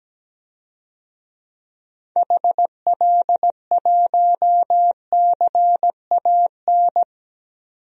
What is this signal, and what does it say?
Morse code, heard as a keyed tone, about 17 wpm, 700 Hz. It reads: HL1CAN
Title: DXCC-100 Call Signs - 6 Characters - 17wpm